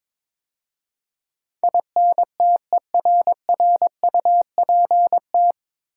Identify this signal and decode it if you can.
Morse code, heard as a keyed tone, about 21 wpm, 700 Hz. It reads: INTERRUPT
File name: 9 Letter Words - First 3 Letters Given - 22wpm